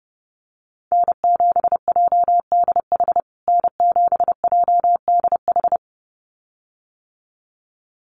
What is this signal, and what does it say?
Morse code, heard as a keyed tone, about 30 wpm, 700 Hz. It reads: N7JDH N7JDH
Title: USA Call Signs Repeated 2x 30wpm